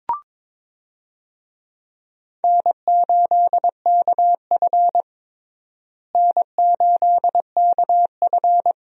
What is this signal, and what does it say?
Morse code, heard as a keyed tone, about 22 wpm, 700 Hz. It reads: N8KF N8KF
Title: USA Call Signs - 4 Characters - Repeated 2x - 22wpm